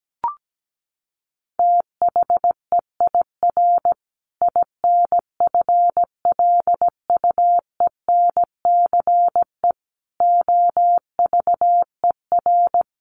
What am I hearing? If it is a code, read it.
Morse code, heard as a keyed tone, about 17 wpm, 700 Hz. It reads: THEIR INFLUENCE OVER
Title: Sentences from Top 1400 Words 17wpm